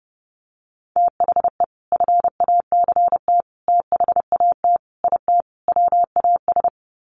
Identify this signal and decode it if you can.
Morse code, heard as a keyed tone, about 30 wpm, 700 Hz. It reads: THE FACT THAT IT WAS
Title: Sets of 5 Words Encore2 30wpm